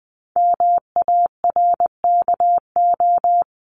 Morse code, heard as a keyed tone, about 20 wpm, 700 Hz. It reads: MARKO
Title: CWT Exchange 20wpm